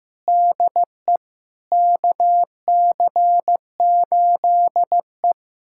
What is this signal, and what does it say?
Morse code, heard as a keyed tone, about 15 wpm, 700 Hz. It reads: DE KC8E
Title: USA Callsign DE Callsign - 4 Characters - 15wpm